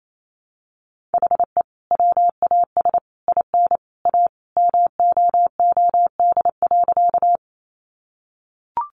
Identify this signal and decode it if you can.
Morse code, heard as a keyed tone, about 28 wpm, 700 Hz. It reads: HE WAS IN A MOOD.
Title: Sentences from Top 1100 Words 28wpm